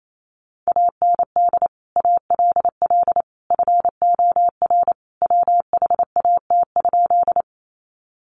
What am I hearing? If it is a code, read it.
Morse code, heard as a keyed tone, about 28 wpm, 700 Hz. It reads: AND ALL FOR WHAT?
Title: Sentences from Top 100 Words 28wpm